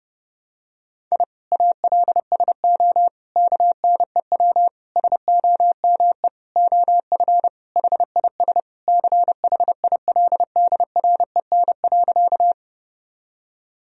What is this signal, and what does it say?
Morse code, heard as a keyed tone, about 29 wpm, 700 Hz. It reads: I ALSO KNEW SOME OF HIS CHILDREN.
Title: Sentences from Top 1000 Words - Review Part 1 - 30wpm